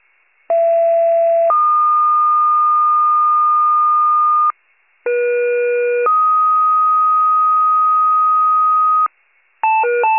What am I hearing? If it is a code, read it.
Two-tone page, heard as a one-second tone, then a three-second tone, about 670 then 1220 Hz, about 480 then 1250 Hz. It is heard from 0.5 to 9.1 s.
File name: pager / 39 RESCUE / wcalert